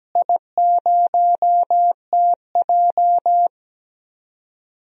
Morse code, heard as a keyed tone, about 17 wpm, 700 Hz. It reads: I0TJ
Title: DXCC-100 Call Signs - 4 Characters - Repeated 3x - 17wpm